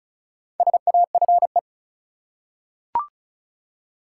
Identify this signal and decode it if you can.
Morse code, heard as a keyed tone, about 36 wpm, 700 Hz. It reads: SAFE
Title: Four Letter Words - 35wpm